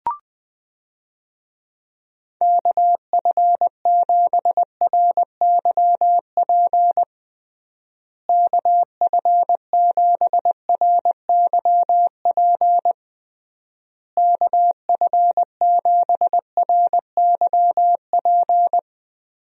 Morse code, heard as a keyed tone, about 20 wpm, 700 Hz. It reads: KF7RYP KF7RYP KF7RYP
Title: USA Call Signs - 6 Characters - Repeated 3x - 20wpm